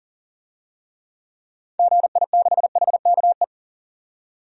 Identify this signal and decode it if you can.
Morse code, heard as a keyed tone, about 40 wpm, 700 Hz. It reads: GI6HKE
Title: DXCC-100 Call Signs - 6 Characters - 40wpm